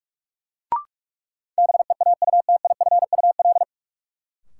Morse code, heard as a keyed tone, about 43 wpm, 700 Hz. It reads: BEAUTIFUL
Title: Sentences from Top 900 Words - New Word Drill - 45wpm